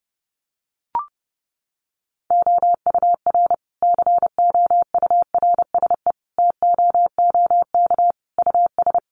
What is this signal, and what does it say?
Morse code, heard as a keyed tone, about 30 wpm, 700 Hz. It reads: OUR COURSE TOOK US
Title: Sentences from Top 1300 Words 30wpm